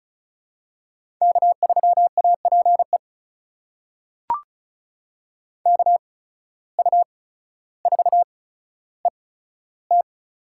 Morse code, heard as a keyed tone, about 36 wpm, spaced as for 15 wpm, 700 Hz. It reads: K3APE KU4ET
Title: USA Call Signs - 5 Characters - 8x Character Spacing - 35wpm